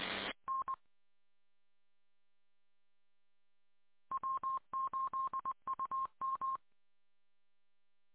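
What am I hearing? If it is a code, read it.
Morse code, heard as a keyed tone, about 20 wpm, 1060 Hz. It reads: N W8UM
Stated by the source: Morse code from W8UM on 145.230.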